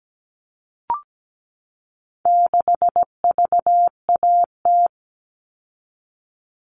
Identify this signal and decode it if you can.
Morse code, heard as a keyed tone, about 17 wpm, 700 Hz. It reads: 6VAT
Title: All Cut Numbers - 4-Digit 17wpm